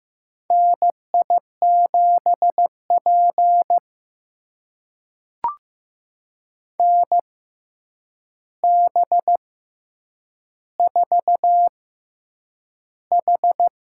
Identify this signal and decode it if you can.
Morse code, heard as a keyed tone, about 15 wpm, spaced as for 8 wpm, 700 Hz. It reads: NI7P NB4H
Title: USA Call Signs - 4 Characters - 6x Character Spacing - 15wpm